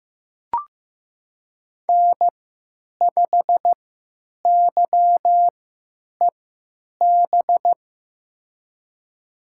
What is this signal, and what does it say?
Morse code, heard as a keyed tone, about 15 wpm, spaced as for 9 wpm, 700 Hz. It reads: N5YEB
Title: USA Call Signs - 3x Character Spacing - 15wpm